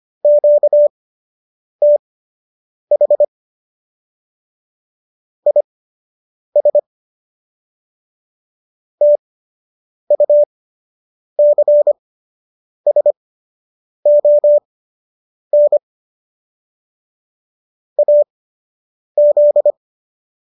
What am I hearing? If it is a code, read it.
Morse code, heard as a keyed tone, about 25 wpm, spaced as for 8 wpm, 590 Hz. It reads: QTH IS TUCSON AZ
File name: qso006 08